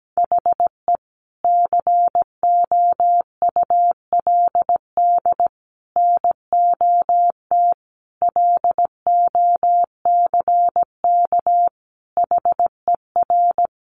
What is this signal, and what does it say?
Morse code, heard as a keyed tone, about 17 wpm, 700 Hz. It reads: HE COULD NOT LOCK HER
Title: Sentences from Top 1300 Words 17wpm